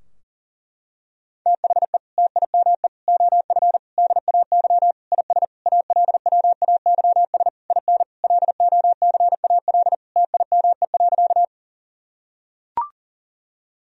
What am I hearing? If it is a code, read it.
Morse code, heard as a keyed tone, about 37 wpm, 700 Hz. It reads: THE TIME OF DAY IS ALWAYS IN LOCAL TIME.
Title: Sentences from Top 1000 Words - Review Part 3 - 40wpm